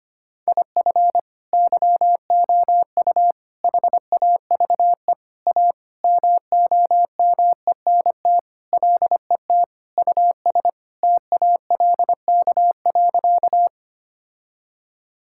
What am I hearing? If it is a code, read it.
Morse code, heard as a keyed tone, about 25 wpm, 700 Hz. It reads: IF YOU HAVE A MOMENT LET US TALK.
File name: practice / 25wpm / Sentences with 4-Word Phrases - 25wpm